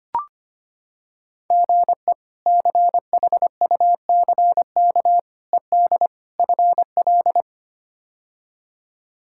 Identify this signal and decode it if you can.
Morse code, heard as a keyed tone, about 25 wpm, 700 Hz. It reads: GE CHUCK ED FL
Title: K1USN SST QSO Practice - 25wpm